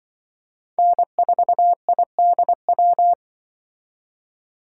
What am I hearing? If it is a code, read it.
Morse code, heard as a keyed tone, about 24 wpm, 700 Hz. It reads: N4IDW
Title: Speed Racing - USA 1x3 Call Signs 20wpm